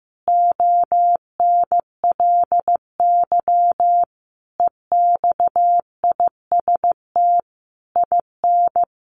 Morse code, heard as a keyed tone, about 15 wpm, 700 Hz. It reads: ONLY EXIST IN
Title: Sentences from Top 1300 Words 15wpm